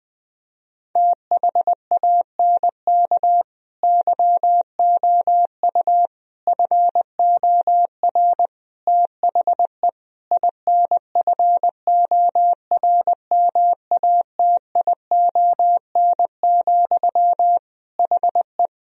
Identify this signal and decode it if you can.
Morse code, heard as a keyed tone, about 21 wpm, 700 Hz. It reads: THANK YOU FOR THE INFORMATION, HE
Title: Sentences from Top 1000 Words - Review Part 3 - 20wpm